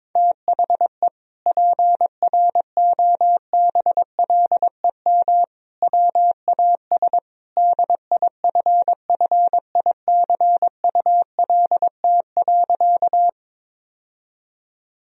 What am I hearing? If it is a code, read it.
Morse code, heard as a keyed tone, about 22 wpm, 700 Hz. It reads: THE PROBLEM WAS DIFFICULT.
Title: Sentences from Top 1000 Words - Review Part 2 - 22wpm